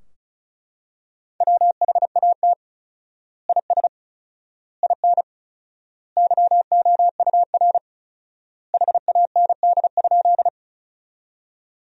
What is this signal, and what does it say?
Morse code, heard as a keyed tone, about 34 wpm, 700 Hz. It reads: WHAT IS IN YOUR HAND?
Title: 4x Word-Spacing - Sentences from Top 200 Words - 35wpm